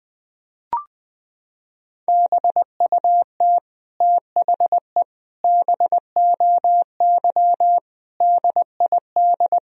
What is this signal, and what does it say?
Morse code, heard as a keyed tone, about 20 wpm, 700 Hz. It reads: BUT THE BOY DID
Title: Sentences from Top 1000 Words - Review Part 3 - 20wpm